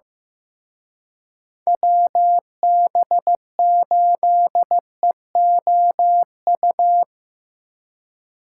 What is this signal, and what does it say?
Morse code, heard as a keyed tone, about 17 wpm, 700 Hz. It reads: WB8EOU
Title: Speed Racing - USA 2x3 Call Signs 15wpm